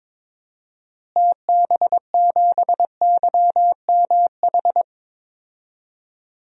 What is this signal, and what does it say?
Morse code, heard as a keyed tone, about 22 wpm, 700 Hz. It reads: TB7YMH
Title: DXCC-100 Call Signs - 6 Characters - Repeated 3x - 22wpm